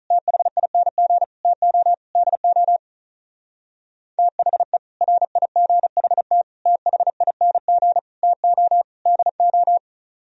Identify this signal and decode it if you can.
Morse code, heard as a keyed tone, about 43 wpm, 700 Hz. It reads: THING TO DO THE RIGHT THING TO DO
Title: Speed-Racing - Sets of 5 Words 35wpm